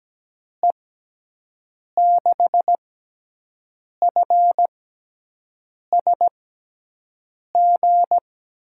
Morse code, heard as a keyed tone, about 17 wpm, spaced as for 6 wpm, 700 Hz. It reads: E6FSG
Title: USA Call Signs - 6 Characters - 6x Character Spacing - 17wpm